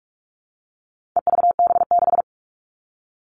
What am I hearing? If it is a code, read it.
Morse code, heard as a keyed tone, about 44 wpm, 700 Hz. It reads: EVB6
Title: All Cut Numbers - 4-Digit 45wpm